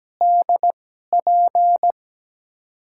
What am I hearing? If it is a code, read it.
Morse code, heard as a keyed tone, about 17 wpm, spaced as for 12 wpm, 700 Hz. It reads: DP
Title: USA Call Signs - 5 Characters - 2x Character Spacing - 17wpm